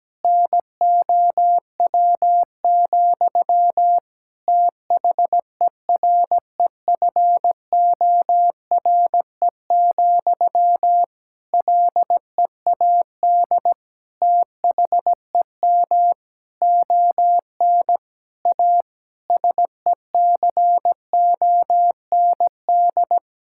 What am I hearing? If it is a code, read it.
Morse code, heard as a keyed tone, about 17 wpm, 700 Hz. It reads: NOW, THEREFORE, LEAD THEM ON A SECOND
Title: Sentences from Top 1000 Words - Review Part 5 - 17wpm